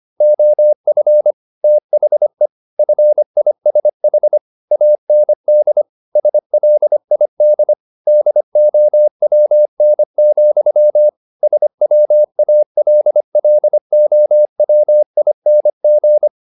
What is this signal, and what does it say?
Morse code, heard as a keyed tone, about 25 wpm, 600 Hz. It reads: OF THE FISH AND SLID DOWN, SWALLOWING